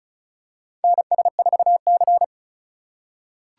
Morse code, heard as a keyed tone, about 34 wpm, 700 Hz. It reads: NS4C
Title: USA Call Signs - 3x Character Spacing - 35wpm